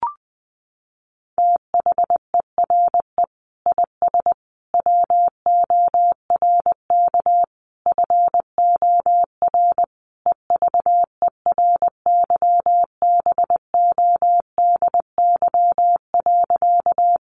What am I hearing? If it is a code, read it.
Morse code, heard as a keyed tone, about 19 wpm, 700 Hz. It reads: THERE IS WORK FOR EVERYBODY.
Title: Sentences from Top 1400 Words 20wpm